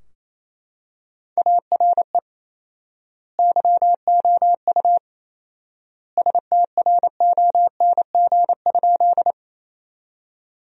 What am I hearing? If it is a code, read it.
Morse code, heard as a keyed tone, about 28 wpm, 700 Hz. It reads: ARE YOU STRONG?
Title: 4x Word-Spacing - Sentences from Top 500 Words - 28wpm